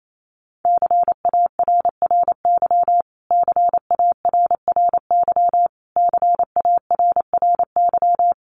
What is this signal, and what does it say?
Morse code, heard as a keyed tone, about 28 wpm, 700 Hz. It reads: CARRY CARRY CARRY
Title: Top 500 Words Repeated 28wpm